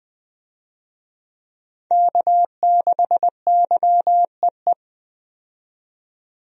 Morse code, heard as a keyed tone, about 21 wpm, 700 Hz. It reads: K6YEE
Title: USA Call Signs - 5 Characters - 20wpm